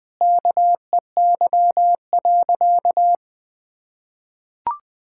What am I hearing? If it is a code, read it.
Morse code, heard as a keyed tone, about 20 wpm, 700 Hz. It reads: KEY.
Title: Sentences with 3-Word Phrases - 20wpm